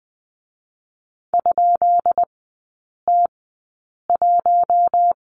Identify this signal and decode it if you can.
Morse code, heard as a keyed tone, about 20 wpm, spaced as for 8 wpm, 700 Hz. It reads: ?T1
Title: 133 Lesson 22e - Question Mark - Three Characters 20wpm